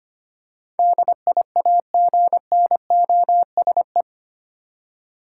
Morse code, heard as a keyed tone, about 25 wpm, 700 Hz. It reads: DIAGNOSE
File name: Eight Letter Words - 25wpm